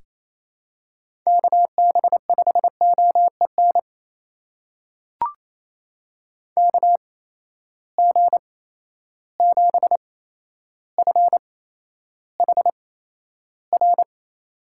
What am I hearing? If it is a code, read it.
Morse code, heard as a keyed tone, about 28 wpm, spaced as for 12 wpm, 700 Hz. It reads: KB5OEN KG7FHR
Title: USA Call Signs - 8x Character Spacing - 28wpm